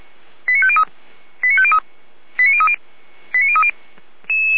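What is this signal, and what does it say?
Five-tone selective call (ZVEI), heard as quick runs of five tones, tones of about 70 ms. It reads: 89503 89503 89030 89030
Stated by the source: ZVEI five-tone selective call received on an SDR, from the Feuerwehr Orlamünde fire brigade server.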